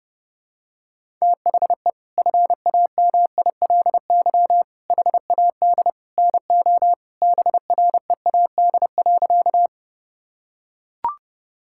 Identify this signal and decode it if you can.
Morse code, heard as a keyed tone, about 30 wpm, 700 Hz. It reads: THE FAMILY HAD NO BREAD.
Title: Sentences from Top 1600 Words 30wpm